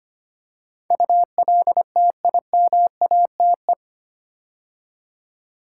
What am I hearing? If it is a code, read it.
Morse code, heard as a keyed tone, about 25 wpm, 690 Hz. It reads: ULTIMATE
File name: 8 Letter Words - First 3 Letters Given - 25wpm